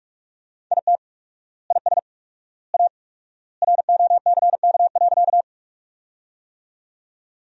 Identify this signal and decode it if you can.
Morse code, heard as a keyed tone, about 46 wpm, 700 Hz. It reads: IT IS A ROCK.
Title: 4x Word-Spacing - Sentences from Top 400 Words - 45wpm